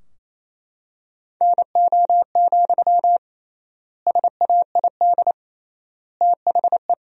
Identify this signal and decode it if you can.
Morse code, heard as a keyed tone, about 27 wpm, 700 Hz. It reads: NO, SAID THE
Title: 3x Word-Spacing - Sentences from Top 300 Words - 28wpm